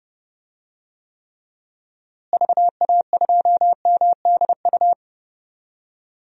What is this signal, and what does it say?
Morse code, heard as a keyed tone, about 30 wpm, 700 Hz. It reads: VA2MDU
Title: DXCC-100 Call Signs - 6 Characters - Repeated 3x - 30wpm